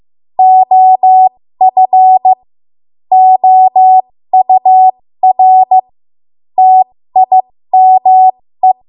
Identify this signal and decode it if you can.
Morse code, heard as a keyed tone, about 15 wpm, spaced as for 13 wpm, 750 Hz. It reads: OF OUR TIME
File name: Archive / 13 WPM / 13WPM